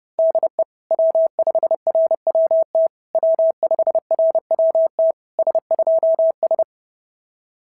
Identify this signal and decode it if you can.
Morse code, heard as a keyed tone, about 30 wpm, 650 Hz. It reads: DE W5RWT W5RWT S2S
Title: SOTA QSO Practice 30wpm